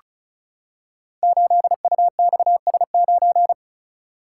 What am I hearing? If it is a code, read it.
Morse code, heard as a keyed tone, about 35 wpm, 700 Hz. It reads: 8UXS9
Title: Code Groups - 5 Letter:Numbers - 35wpm